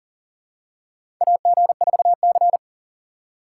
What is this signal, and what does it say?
Morse code, heard as a keyed tone, about 40 wpm, 700 Hz. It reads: AG4C
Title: USA Call Signs - 4 Characters - 3x Character Spacing - 40wpm